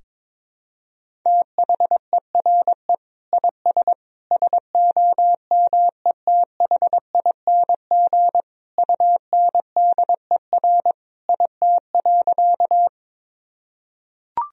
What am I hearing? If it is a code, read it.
Morse code, heard as a keyed tone, about 22 wpm, 700 Hz. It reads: THERE IS SOMETHING UNDER IT.
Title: Sentences from Top 1000 Words - Review Part 1 - 22wpm